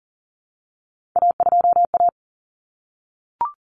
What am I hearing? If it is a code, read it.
Morse code, heard as a keyed tone, about 40 wpm, 700 Hz. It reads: A2A
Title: Common Cut Numbers - 3-Digit 40wpm